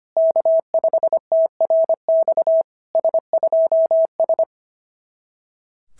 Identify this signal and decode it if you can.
Morse code, heard as a keyed tone, about 25 wpm, 660 Hz. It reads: K5TRX S2S
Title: SOTA QSO Practice 25wpm